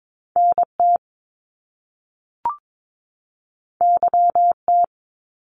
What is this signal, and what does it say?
Morse code, heard as a keyed tone, about 22 wpm, 700 Hz. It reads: NT YT
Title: Canadian Provinces and Territories 22wpm